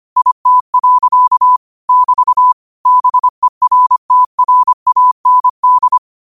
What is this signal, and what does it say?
Morse code, heard as a keyed tone, about 25 wpm, 1000 Hz. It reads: IT. <BT> BERTRAND
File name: qotdfiles / 25wpm